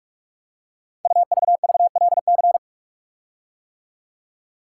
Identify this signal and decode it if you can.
Morse code, heard as a keyed tone, about 44 wpm, 700 Hz. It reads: UVVLC
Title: Code Groups - 5 Letters - 45wpm